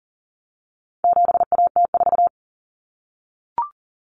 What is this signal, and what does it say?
Morse code, heard as a keyed tone, about 40 wpm, 700 Hz. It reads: 7AT4
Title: Common Cut Numbers - 4-Digit 40wpm